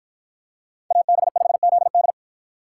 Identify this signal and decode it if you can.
Morse code, heard as a keyed tone, about 48 wpm, 700 Hz. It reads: AB5ZD